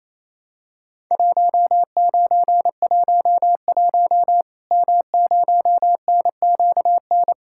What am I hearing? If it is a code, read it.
Morse code, heard as a keyed tone, about 28 wpm, 700 Hz. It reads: 1911 M0NQN
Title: Worked All Europe DX Contest - QTC Exchange - 28wpm